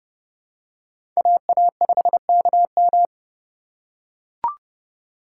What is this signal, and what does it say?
Morse code, heard as a keyed tone, about 30 wpm, 700 Hz. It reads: AA5KM